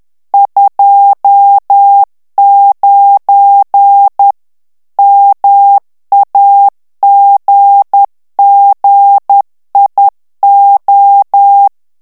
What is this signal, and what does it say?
Morse code, heard as a keyed tone, about 10 wpm, 800 Hz. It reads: 29 MAGGIO
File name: ITALIEN texte clair 1 vitesse 10 mots minute